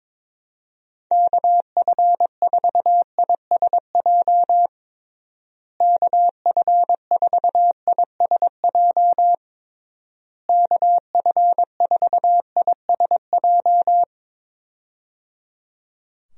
Morse code, heard as a keyed tone, about 21 wpm, 700 Hz. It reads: KF4ISJ KF4ISJ KF4ISJ
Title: USA Call Signs - 6 Characters - Repeated 3x - 22wpm